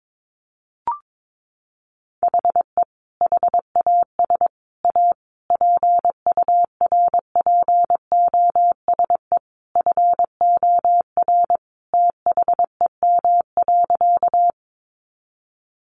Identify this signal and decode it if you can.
Morse code, heard as a keyed tone, about 22 wpm, 700 Hz. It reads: HE HAS A PURPOSE FOR THEM.
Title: Sentences from Top 800 Words 22wpm